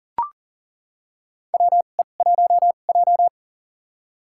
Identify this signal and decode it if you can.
Morse code, heard as a keyed tone, about 37 wpm, spaced as for 28 wpm, 700 Hz. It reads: WE1J
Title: USA Call Signs - 4 Characters - 2x Character Spacing - 40wpm